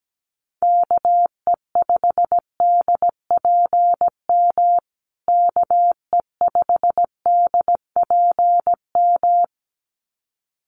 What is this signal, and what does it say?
Morse code, heard as a keyed tone, about 17 wpm, 700 Hz. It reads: KE5DPM KE5DPM
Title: USA Call Signs Repeated 2x 17wpm